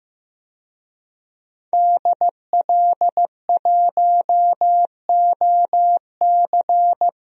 Morse code, heard as a keyed tone, about 15 wpm, 700 Hz. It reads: DL1OC
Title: DXCC-100 Call Signs - 5 Characters - 15wpm